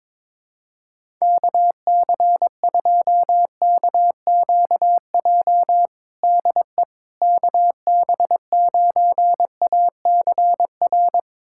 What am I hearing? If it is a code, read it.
Morse code, heard as a keyed tone, about 22 wpm, 700 Hz. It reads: KC2KQJ DE KB9ACR
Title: USA Callsign DE Callsign - 6 Characters - 22wpm